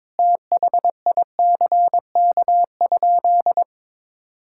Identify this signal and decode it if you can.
Morse code, heard as a keyed tone, about 22 wpm, 700 Hz. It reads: THICK?
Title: Sentences from Top 1600 Words 22wpm